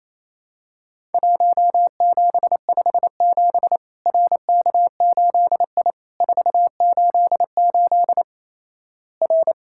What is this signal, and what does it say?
Morse code, heard as a keyed tone, about 28 wpm, 700 Hz. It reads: 1757 RK8I 488 R
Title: Worked All Europe DX Contest - QTC Exchange - 28wpm